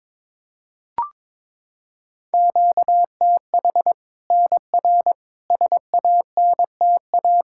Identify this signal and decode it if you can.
Morse code, heard as a keyed tone, about 22 wpm, 700 Hz. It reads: QTH NR SANTA
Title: Practice QSO - Exchange 1 - Beginning - 22wpm